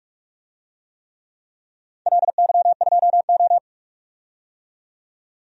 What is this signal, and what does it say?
Morse code, heard as a keyed tone, about 46 wpm, 700 Hz. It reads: LY2O
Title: DXCC-100 Call Signs - 4 Characters - 45wpm